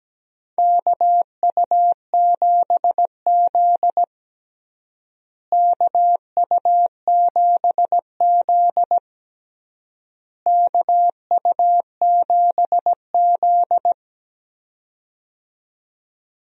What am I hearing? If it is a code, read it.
Morse code, heard as a keyed tone, about 17 wpm, 700 Hz. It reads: KU7Z KU7Z KU7Z
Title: USA Call Signs - 4 Characters - Repeated 3x - 17wpm